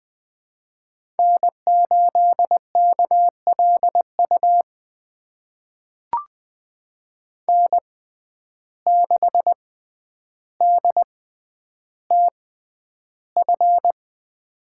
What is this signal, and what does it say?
Morse code, heard as a keyed tone, about 20 wpm, spaced as for 10 wpm, 700 Hz. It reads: N8KLU N6DTF
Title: USA Call Signs - 5 Characters - 6x Character Spacing - 20wpm